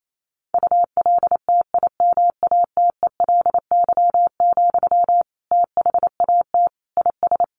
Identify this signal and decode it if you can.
Morse code, heard as a keyed tone, about 28 wpm, 700 Hz. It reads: ULTIMATELY, THAT IS
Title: Sentences from Top 1200 Words 28wpm